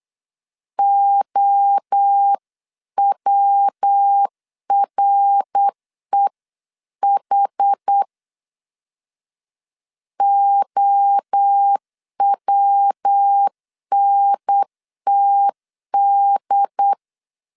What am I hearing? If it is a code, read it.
Morse code, heard as a keyed tone, about 8 wpm, 800 Hz. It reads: OWREH OWNTD